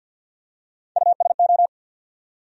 Morse code, heard as a keyed tone, about 54 wpm, 700 Hz. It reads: USO
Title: USA Call Sign Postfixes - 3 Letters 50wpm